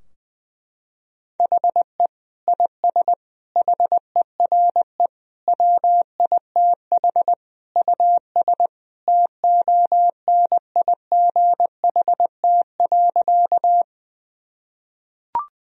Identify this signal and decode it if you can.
Morse code, heard as a keyed tone, about 20 wpm, 700 Hz. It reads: HE IS HERE WITH US TONIGHT.
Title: Sentences from Top 1000 Words - Review Part 3 - 20wpm